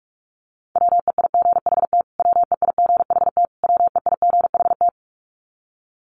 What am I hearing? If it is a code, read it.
Morse code, heard as a keyed tone, about 44 wpm, 700 Hz. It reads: WEIGHT WEIGHT WEIGHT
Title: Top 500 Words Repeated 45wpm